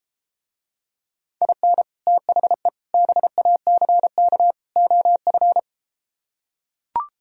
Morse code, heard as a keyed tone, about 34 wpm, 700 Hz. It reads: IN THE BACK OF